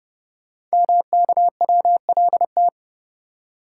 Morse code, heard as a keyed tone, about 30 wpm, 700 Hz. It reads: MKWLT
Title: Code Groups - 5 Letter:Numbers - 30wpm